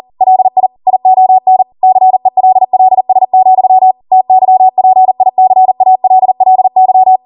Morse code, heard as a keyed tone, about 40 wpm, 750 Hz. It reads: LI ION CELLS, TYPICALLY
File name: Morse / 40WPM